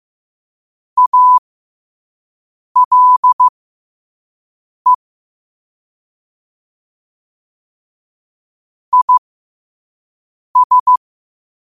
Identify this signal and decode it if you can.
Morse code, heard as a keyed tone, about 15 wpm, spaced as for 5 wpm, 1000 Hz. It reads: ALE IS
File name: qotdfiles / 5wpm